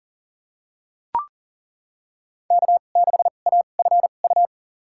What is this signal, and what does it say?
Morse code, heard as a keyed tone, about 40 wpm, spaced as for 29 wpm, 700 Hz. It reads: K6AFU
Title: USA Call Signs - 2x Character Spacing - 40wpm